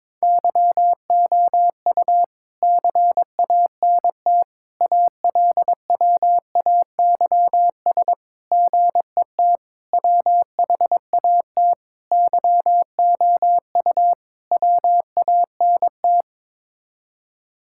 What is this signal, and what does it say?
Morse code, heard as a keyed tone, about 21 wpm, 700 Hz. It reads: YOU CANT ALWAYS GET WHAT YOU WANT
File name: practice / 22wpm / Sets of 7 Words - 22wpm